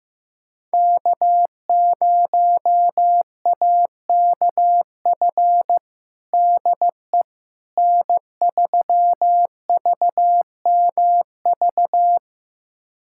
Morse code, heard as a keyed tone, about 15 wpm, 700 Hz. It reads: K0AKF DE N3VMV
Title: USA Callsign DE Callsign - 5 Characters - 15wpm